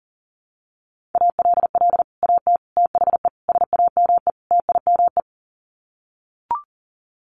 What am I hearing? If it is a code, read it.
Morse code, heard as a keyed tone, about 40 wpm, 700 Hz. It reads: ALL AT THE SAME TIME
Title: Sets of 5 Words Encore2 40wpm